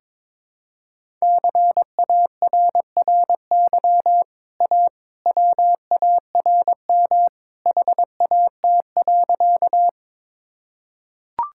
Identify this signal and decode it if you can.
Morse code, heard as a keyed tone, about 22 wpm, 700 Hz. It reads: CARRY A WARM HAT.
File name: practice / 22wpm / Easy Sentences with Hints - 22wpm